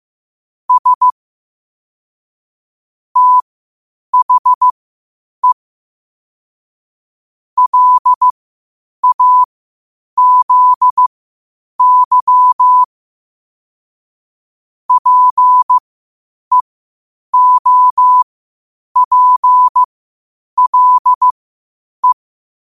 Morse code, heard as a keyed tone, about 15 wpm, spaced as for 8 wpm, 1000 Hz. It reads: S THE LAZY PEOPLE